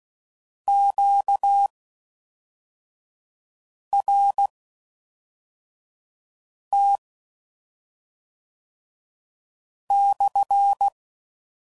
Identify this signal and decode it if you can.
Morse code, heard as a keyed tone, about 16 wpm, spaced as for 3 wpm, 800 Hz. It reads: QRT/